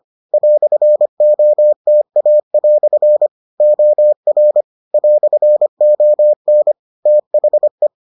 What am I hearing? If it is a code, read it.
Morse code, heard as a keyed tone, about 25 wpm, 600 Hz. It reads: "OTA" OR "ON THE